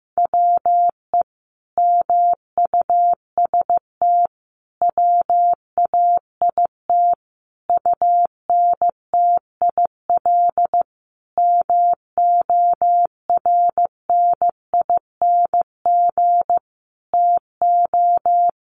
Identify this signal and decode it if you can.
Morse code, heard as a keyed tone, about 14 wpm, 700 Hz. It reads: WE MUST WAIT UNTIL MORNING TO
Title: Sentences from Top 1300 Words 15wpm